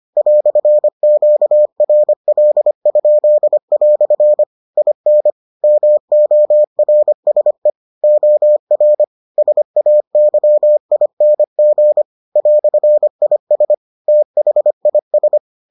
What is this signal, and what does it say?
Morse code, heard as a keyed tone, about 25 wpm, 600 Hz. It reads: "QRL?" IN MORSE OR SAYING "IS THIS